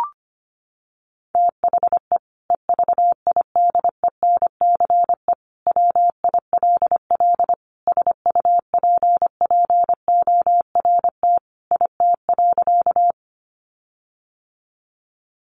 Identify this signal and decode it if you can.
Morse code, heard as a keyed tone, about 25 wpm, 700 Hz. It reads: THE EVIDENCE WILL SUPPORT IT.
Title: Sentences from Top 1300 Words 25wpm